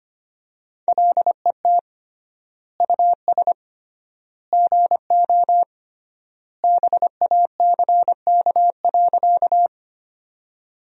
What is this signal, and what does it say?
Morse code, heard as a keyed tone, about 25 wpm, 700 Hz. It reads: LET US GO BACK.
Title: 3x Word-Spacing - Sentences from Top 300 Words - 25wpm